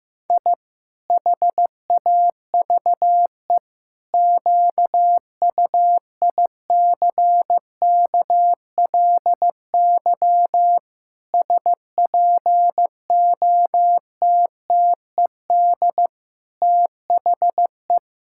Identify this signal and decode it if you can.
Morse code, heard as a keyed tone, about 16 wpm, 700 Hz. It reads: I HAVE QUICKLY SPOTTED THE
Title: Pangram Challenge v2 - 15wpm